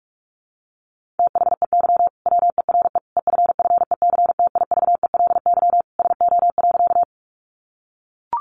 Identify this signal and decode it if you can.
Morse code, heard as a keyed tone, about 44 wpm, 700 Hz. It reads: THEY WERE EFFECTIVELY SO.
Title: Sentences from Top 1400 Words 45wpm